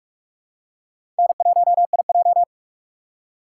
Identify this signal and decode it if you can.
Morse code, heard as a keyed tone, about 46 wpm, 700 Hz. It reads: N1IJ
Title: USA Call Signs - 4 Characters - 3x Character Spacing - 45wpm